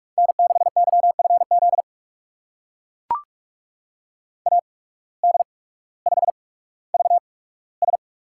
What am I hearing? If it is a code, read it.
Morse code, heard as a keyed tone, about 44 wpm, spaced as for 19 wpm, 700 Hz. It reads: N6YFZ AD5VS
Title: USA Call Signs - 8x Character Spacing - 45wpm